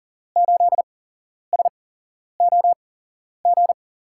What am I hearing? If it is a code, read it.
Morse code, heard as a keyed tone, about 38 wpm, spaced as for 11 wpm, 700 Hz. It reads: 8SOG
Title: USA Call Signs - 6 Characters - 8x Character Spacing - 40wpm